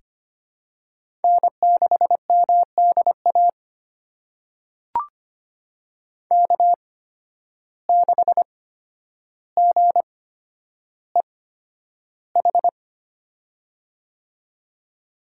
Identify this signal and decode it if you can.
Morse code, heard as a keyed tone, about 25 wpm, spaced as for 11 wpm, 700 Hz. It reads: N6MDA K6GEH
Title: USA Call Signs - 5 Characters - 8x Character Spacing - 25wpm